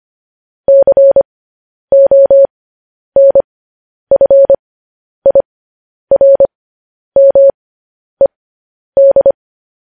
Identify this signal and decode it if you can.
Morse code, heard as a keyed tone, about 25 wpm, spaced as for 10 wpm, 550 Hz. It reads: CONFIRMED
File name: podcast / News headlines-1x-25x10